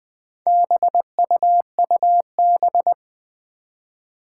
Morse code, heard as a keyed tone, about 20 wpm, 700 Hz. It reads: BUUB
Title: Four Challenging Letter-Number Combinations 20wpm